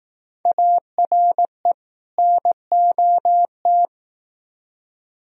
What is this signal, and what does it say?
Morse code, heard as a keyed tone, about 18 wpm, 700 Hz. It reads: ARE NOT
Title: Speed-Racing - Sets of 2 Words 15wpm